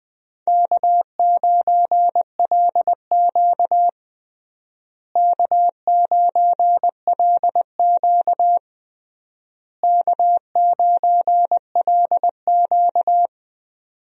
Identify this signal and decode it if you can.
Morse code, heard as a keyed tone, about 19 wpm, 700 Hz. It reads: K9LQ K9LQ K9LQ
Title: USA Call Signs - 4 Characters - Repeated 3x - 20wpm